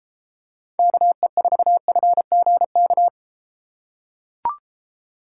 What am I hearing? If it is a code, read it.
Morse code, heard as a keyed tone, about 31 wpm, 700 Hz. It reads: KE4FGK